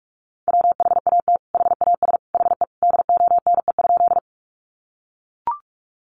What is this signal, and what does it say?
Morse code, heard as a keyed tone, about 44 wpm, 700 Hz. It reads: WHAT HAS HE DONE?
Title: Sentences from Top 500 Words 45wpm